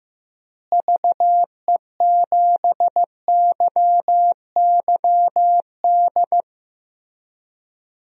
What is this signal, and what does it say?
Morse code, heard as a keyed tone, about 15 wpm, 700 Hz. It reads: VE7YYD
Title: DXCC-100 Call Signs - 6 Characters - Repeated 2x - 15wpm